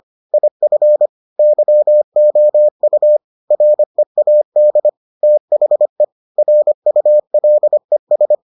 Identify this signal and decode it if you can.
Morse code, heard as a keyed tone, about 25 wpm, 600 Hz. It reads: IF YOU READ THE RULES